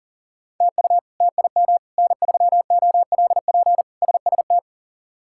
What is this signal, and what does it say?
Morse code, heard as a keyed tone, about 40 wpm, 700 Hz. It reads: TU TIM N3OLP SST
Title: K1USN SST QSO Practice - 40wpm